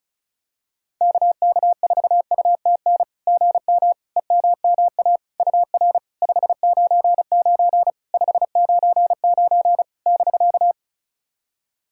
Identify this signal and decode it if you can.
Morse code, heard as a keyed tone, about 36 wpm, 700 Hz. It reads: KK4UTN GM EMMA UR 599 599 <BK>
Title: SOTA QSO Practice 35wpm